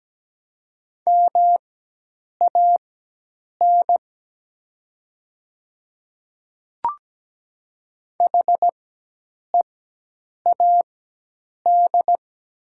Morse code, heard as a keyed tone, about 17 wpm, spaced as for 8 wpm, 700 Hz. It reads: MAN HEAD
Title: 4x Character Spacing - Top 300 Words - 17wpm